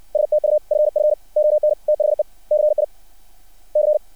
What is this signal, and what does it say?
Morse code, heard as a keyed tone, about 60 wpm, spaced as for 37 wpm, 600 Hz. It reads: STUFF ON THE CIC
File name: archived / CRSnet